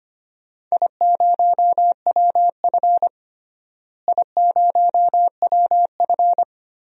Morse code, heard as a keyed tone, about 25 wpm, 700 Hz. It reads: I0WF I0WF
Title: DXCC-100 Call Signs - 4 Characters - Repeated 2x - 25wpm